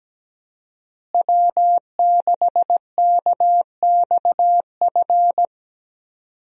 Morse code, heard as a keyed tone, about 17 wpm, 700 Hz. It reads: W6KXF
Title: USA Call Signs - 5 Characters - Repeated 2x - 17wpm